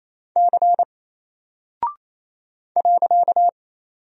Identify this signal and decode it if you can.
Morse code, heard as a keyed tone, about 28 wpm, 700 Hz. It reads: C .
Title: Single Character (Letter:Number:Punctuation:Prosign) Alt 28wpm